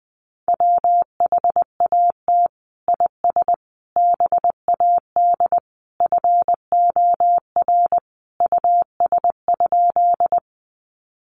Morse code, heard as a keyed tone, about 20 wpm, 700 Hz. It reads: WHAT IS BAD FOR US?
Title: Sentences from Top 700 Words 20wpm